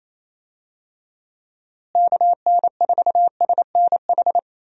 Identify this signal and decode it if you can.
Morse code, heard as a keyed tone, about 28 wpm, 700 Hz. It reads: KN4SNH
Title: USA Call Signs - 6 Characters - Repeated 2x - 28wpm